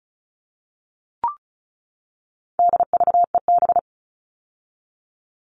Morse code, heard as a keyed tone, about 34 wpm, 700 Hz. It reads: DVEB
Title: All Cut Numbers - 4-Digit 35wpm